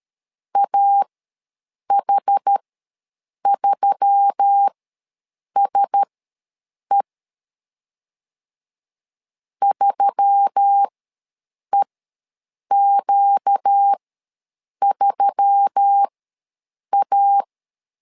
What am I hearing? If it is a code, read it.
Morse code, heard as a keyed tone, about 13 wpm, spaced as for 7 wpm, 800 Hz. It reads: AH3SE 3EQ3A
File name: LEZIONI / 4 Q1S3 ws 12wpm 2038